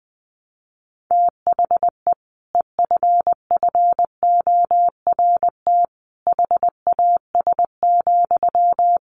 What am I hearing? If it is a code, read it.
Morse code, heard as a keyed tone, about 20 wpm, 700 Hz. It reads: THE EFFORT HAS,
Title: Sentences from Top 1200 Words 20wpm